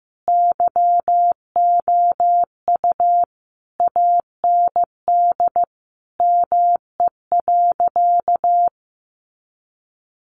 Morse code, heard as a keyed tone, about 15 wpm, 700 Hz. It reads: YOU AND ME.
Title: Sentences from Top 1200 Words 15wpm